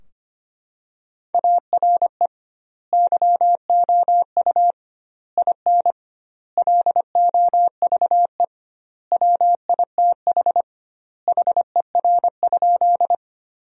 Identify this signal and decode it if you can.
Morse code, heard as a keyed tone, about 25 wpm, 700 Hz. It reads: ARE YOU IN LOVE WITH HER?
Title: 2x Word-Spacing - Sentences from Top 400 Words - 25wpm